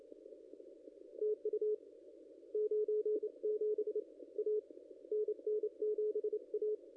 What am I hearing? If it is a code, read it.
Morse code, heard as a keyed tone, about 28 wpm, 440 Hz. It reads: TU 97 A NN7A
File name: contests / ss-cw-k5tr-op-n5rz -0100